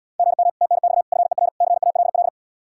Morse code, heard as a keyed tone, about 80 wpm, 700 Hz. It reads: GM TNX FER CALL